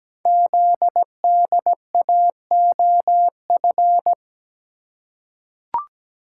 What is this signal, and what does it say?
Morse code, heard as a keyed tone, about 17 wpm, 700 Hz. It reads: ZDAOF
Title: Code Groups - 5 Letter:Numbers - 17wpm